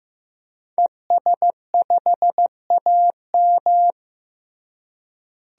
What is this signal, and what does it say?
Morse code, heard as a keyed tone, about 15 wpm, 700 Hz. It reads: ES5AM
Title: DXCC-100 Call Signs - 5 Characters - Repeated 3x - 15wpm